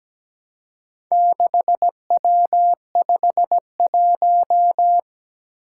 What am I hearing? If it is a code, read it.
Morse code, heard as a keyed tone, about 17 wpm, 700 Hz. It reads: 6W51
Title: Four Challenging Letter-Number Combinations 17wpm